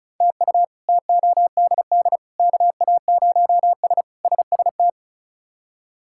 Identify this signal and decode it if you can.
Morse code, heard as a keyed tone, about 34 wpm, 700 Hz. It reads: TU TODD KA0S SST
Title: K1USN SST QSO Practice - 35wpm